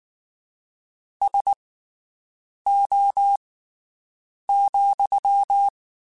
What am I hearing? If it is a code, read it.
Morse code, heard as a keyed tone, about 19 wpm, spaced as for 7 wpm, 800 Hz. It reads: SO,
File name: QSOs Spaced